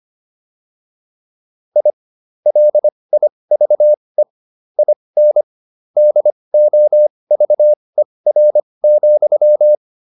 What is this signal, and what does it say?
Morse code, heard as a keyed tone, about 25 wpm, spaced as for 20 wpm, 600 Hz. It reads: I LIVE IN DOVER,
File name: QSO308a 20